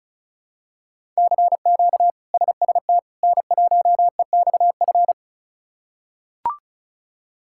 Morse code, heard as a keyed tone, about 36 wpm, 700 Hz. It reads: CQ SST N1EXF
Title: K1USN SST QSO Practice - 35wpm